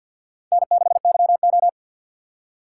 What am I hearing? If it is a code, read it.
Morse code, heard as a keyed tone, about 50 wpm, 700 Hz. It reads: N6YO
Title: USA Call Signs - 4 Characters - 6x Character Spacing - 50wpm